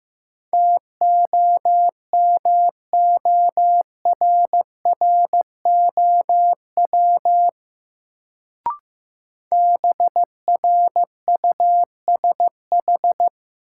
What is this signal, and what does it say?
Morse code, heard as a keyed tone, about 15 wpm, 700 Hz. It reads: TOMORROW BRUSH
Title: Sentences from Top 1100 Words - New Word Drill Alt - 15wpm